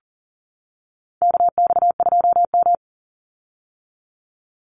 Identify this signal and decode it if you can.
Morse code, heard as a keyed tone, about 40 wpm, 700 Hz. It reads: KX2M
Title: USA Call Signs 40wpm